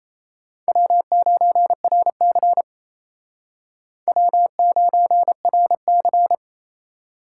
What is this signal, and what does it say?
Morse code, heard as a keyed tone, about 34 wpm, 700 Hz. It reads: W9RC W9RC
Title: Speed Racing - USA 1x2 Call Signs 28wpm